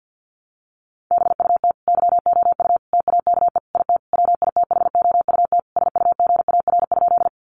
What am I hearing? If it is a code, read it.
Morse code, heard as a keyed tone, about 49 wpm, 700 Hz. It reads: BUT YOU TAKE IT WITHOUT SUGAR?
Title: Sentences from Top 1200 Words 50wpm